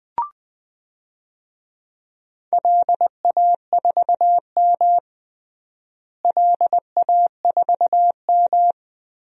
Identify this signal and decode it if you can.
Morse code, heard as a keyed tone, about 20 wpm, 700 Hz. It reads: LA4M LA4M
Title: DXCC-100 Call Signs - 4 Characters - Repeated 3x - 20wpm